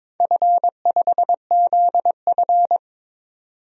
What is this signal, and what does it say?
Morse code, heard as a keyed tone, about 22 wpm, 700 Hz. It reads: F5ZF
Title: DXCC-100 Call Signs - 4 Characters - 22wpm